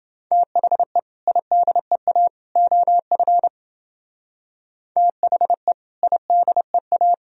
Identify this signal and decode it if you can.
Morse code, heard as a keyed tone, about 29 wpm, 700 Hz. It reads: THE IDEA OF THE IDEA